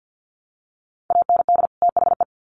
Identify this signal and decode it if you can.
Morse code, heard as a keyed tone, about 48 wpm, 700 Hz. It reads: AND THE
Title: Sets of 2 Words 50wpm